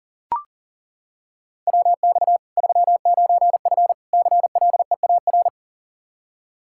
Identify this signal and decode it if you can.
Morse code, heard as a keyed tone, about 40 wpm, 700 Hz. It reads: WX 39F CLEAR
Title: Practice QSO - Exchange 2 - Info - 40wpm